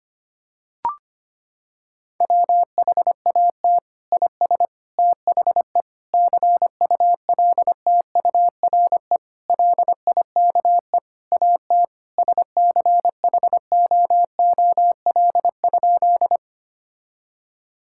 Morse code, heard as a keyed tone, about 25 wpm, 700 Hz. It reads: WHAT IS THE CULTURE LIKE AT SCHOOL?
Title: Sentences from Top 1000 Words - Review Part 6 - 25wpm